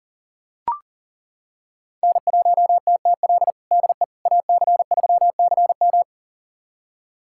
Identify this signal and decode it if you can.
Morse code, heard as a keyed tone, about 40 wpm, 700 Hz. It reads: N1TTL DE AC3CM
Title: USA Callsign DE Callsign - 5 Characters - 40wpm